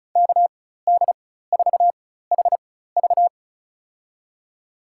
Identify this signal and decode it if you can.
Morse code, heard as a keyed tone, about 34 wpm, spaced as for 16 wpm, 700 Hz. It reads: KD4HV
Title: USA Call Signs - 5 Characters - 4x Character Spacing - 35wpm